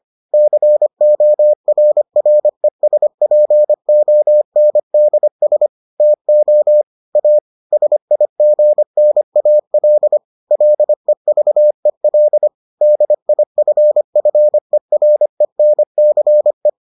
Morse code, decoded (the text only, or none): CORRESPONDS TO A SIGNAL LEVEL DIFFERENCE